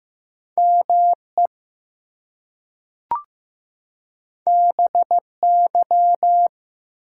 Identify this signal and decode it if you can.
Morse code, heard as a keyed tone, about 15 wpm, 700 Hz. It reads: ME BY
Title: Two Letter Words - 15wpm